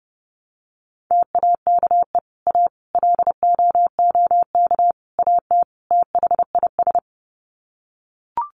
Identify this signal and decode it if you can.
Morse code, heard as a keyed tone, about 30 wpm, 700 Hz. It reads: TAKE A LOOK AT THIS
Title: Sets of 5 Words 30wpm